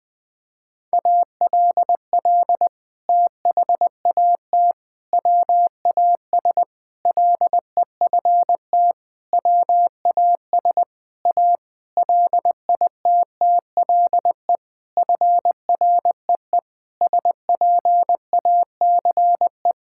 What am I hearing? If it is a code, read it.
Morse code, heard as a keyed tone, about 21 wpm, 700 Hz. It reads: ALL THAT WAS LEFT WAS A LITTLE FREE SPACE
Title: Sentences from Top 1000 Words - Review Part 5 - 20wpm